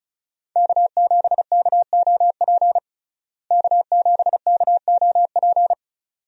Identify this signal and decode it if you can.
Morse code, heard as a keyed tone, about 34 wpm, 700 Hz. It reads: K7KOP K7KOP
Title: USA Call Signs - 5 Characters - Repeated 2x - 35wpm